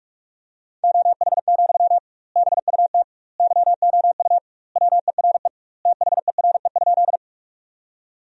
Morse code, heard as a keyed tone, about 46 wpm, 700 Hz. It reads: OH, BUT YOU WERE THERE?
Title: 2x Word-Spacing - Sentences from Top 500 Words - 45wpm